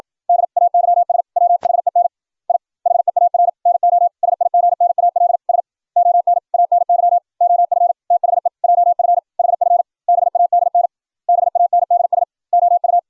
Morse code, heard as a keyed tone, about 56 wpm, 700 Hz. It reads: DAYS WHEN I HEAR NO SIGNALS ON ANY OF THE JF HF BADN BANDS OU